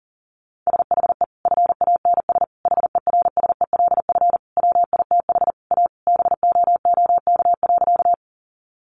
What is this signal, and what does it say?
Morse code, heard as a keyed tone, about 40 wpm, 700 Hz. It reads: SHE FANS HERSELF WITH A BOOK.
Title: Easy Phrases 40wpm